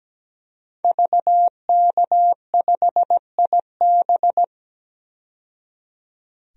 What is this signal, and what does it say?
Morse code, heard as a keyed tone, about 17 wpm, 700 Hz. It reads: VK5IB
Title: DXCC-100 Call Signs - 5 Characters - 17wpm